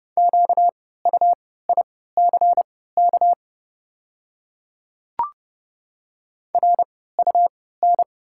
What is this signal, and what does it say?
Morse code, heard as a keyed tone, about 30 wpm, spaced as for 17 wpm, 700 Hz. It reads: QUICK RUN
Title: 3x Character Spacing - Top 500 Words - 30wpm